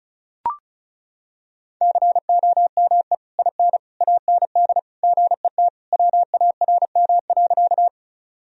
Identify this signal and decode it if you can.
Morse code, heard as a keyed tone, about 34 wpm, 700 Hz. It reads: COME IN AND GET WARM.
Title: Sentences from Top 1000 Words - Review Part 2 - 35wpm